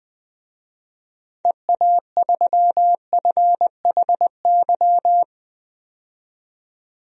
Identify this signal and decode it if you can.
Morse code, heard as a keyed tone, about 21 wpm, 700 Hz. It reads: EA3FHY
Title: DXCC-100 Call Signs - 6 Characters - 20wpm